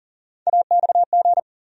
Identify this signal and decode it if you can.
Morse code, heard as a keyed tone, about 40 wpm, 700 Hz. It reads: AXG
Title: USA Call Sign Postfixes - 3 Letters 40wpm